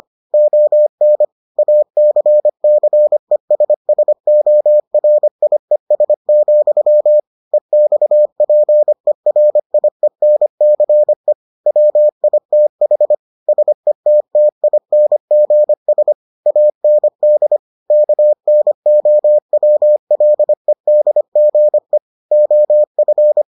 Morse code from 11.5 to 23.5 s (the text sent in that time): WITH SETTINGS AND KNOWLEDGE OF